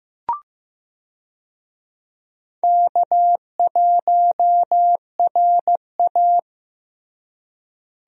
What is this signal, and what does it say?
Morse code, heard as a keyed tone, about 15 wpm, 700 Hz. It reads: K1RA
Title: USA Call Signs - 4 Characters - Repeated 2x - 15wpm